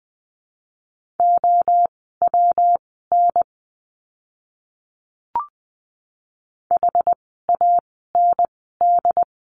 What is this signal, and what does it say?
Morse code, heard as a keyed tone, about 20 wpm, spaced as for 15 wpm, 700 Hz. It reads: OWN HAND
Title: 115 Lesson 19f - Letter w - Words 20wpm